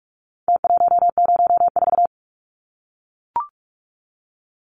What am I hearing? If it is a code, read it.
Morse code, heard as a keyed tone, about 44 wpm, 700 Hz. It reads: T104
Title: Numbers - Contest Style - 4-Digit 45wpm